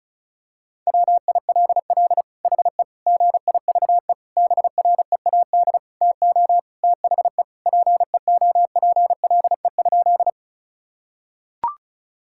Morse code, heard as a keyed tone, about 34 wpm, 700 Hz. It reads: WILL HE GIVE BREAD TO THE PEOPLE?
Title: Sentences from Top 1600 Words 35wpm